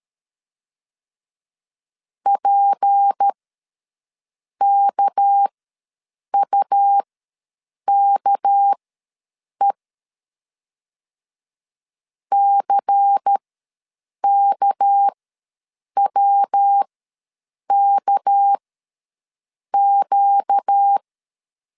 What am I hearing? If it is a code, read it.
Morse code, heard as a keyed tone, about 13 wpm, spaced as for 7 wpm, 800 Hz. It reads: PKUKE CKWKQ
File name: LEZIONI / 6 PKX0 / 6 PK ws 12wpm 3116